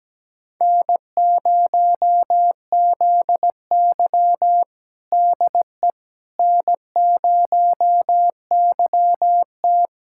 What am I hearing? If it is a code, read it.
Morse code, heard as a keyed tone, about 17 wpm, 700 Hz. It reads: N0ZY DE N0YT
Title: USA Callsign DE Callsign - 4 Characters - 17wpm